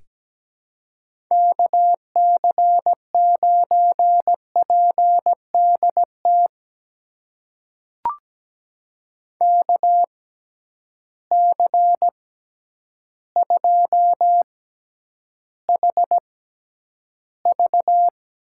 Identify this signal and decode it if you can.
Morse code, heard as a keyed tone, about 17 wpm, spaced as for 9 wpm, 700 Hz. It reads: KC9PDT KC2HV
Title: USA Call Signs - 6 Characters - 6x Character Spacing - 17wpm